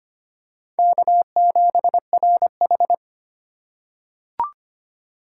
Morse code, heard as a keyed tone, about 25 wpm, 700 Hz. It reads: K7RH
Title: USA Call Signs - 4 Characters - 4x Character Spacing - 25wpm